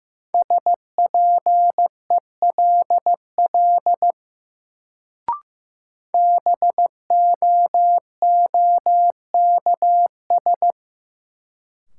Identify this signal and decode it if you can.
Morse code, heard as a keyed tone, about 15 wpm, 700 Hz. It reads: SPELL BOOKS
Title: Sentences from Top 1200 Words - New Word Drill Alt - 15wpm